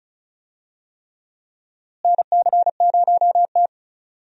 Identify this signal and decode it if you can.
Morse code, heard as a keyed tone, about 36 wpm, 700 Hz. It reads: NC0T
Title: USA Call Signs - 4 Characters - Repeated 3x - 35wpm